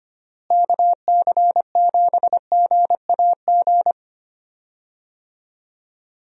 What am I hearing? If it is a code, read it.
Morse code, heard as a keyed tone, about 25 wpm, 700 Hz. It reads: KC7GAG
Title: USA Call Signs - 6 Characters - Repeated 2x - 25wpm